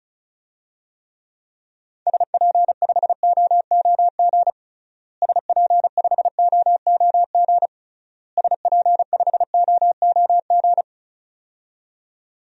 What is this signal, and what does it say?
Morse code, heard as a keyed tone, about 36 wpm, 700 Hz. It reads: SP5OOG SP5OOG SP5OOG
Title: DXCC-100 Call Signs - 6 Characters - Repeated 3x - 35wpm